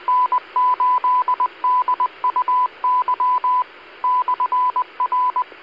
Morse code, heard as a keyed tone, about 20 wpm, 1000 Hz, with radio noise. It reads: N8DUY/R
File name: Washtenaw County SKYWARN N8DUY and W8SRC Repeaters